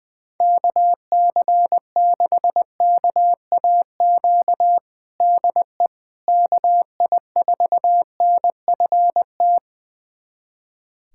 Morse code, heard as a keyed tone, about 20 wpm, 700 Hz. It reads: KC6KAQ DE KI4NFT
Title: USA Callsign DE Callsign - 6 Characters - 20wpm